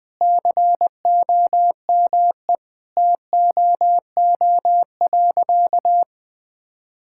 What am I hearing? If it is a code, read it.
Morse code, heard as a keyed tone, about 20 wpm, 700 Hz. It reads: COME TOO.
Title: Sentences with 3-Word Phrases - Spoken Beginning - 20wpm